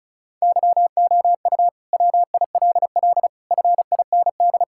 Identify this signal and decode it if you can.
Morse code, heard as a keyed tone, about 36 wpm, 700 Hz. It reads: YOU WILL FIND
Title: Sentences from Top 1000 Words - Review Part 5 - 35wpm